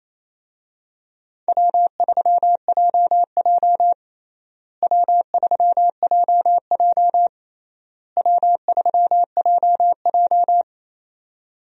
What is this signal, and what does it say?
Morse code, heard as a keyed tone, about 28 wpm, 700 Hz. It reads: W3JJ W3JJ W3JJ
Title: USA Call Signs - 4 Characters - Repeated 3x - 28wpm